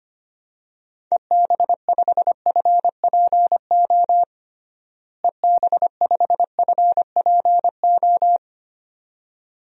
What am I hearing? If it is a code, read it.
Morse code, heard as a keyed tone, about 25 wpm, 700 Hz. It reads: EB5FPO EB5FPO
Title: DXCC-100 Call Signs - 6 Characters - Repeated 2x - 25wpm